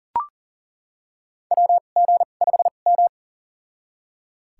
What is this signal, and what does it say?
Morse code, heard as a keyed tone, about 40 wpm, spaced as for 29 wpm, 700 Hz. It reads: WG5M
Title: USA Call Signs - 2x Character Spacing - 40wpm